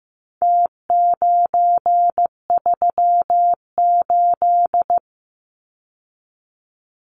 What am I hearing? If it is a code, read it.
Morse code, heard as a keyed tone, about 15 wpm, 700 Hz. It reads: T938
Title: Numbers - Contest Style - 4-Digit 15wpm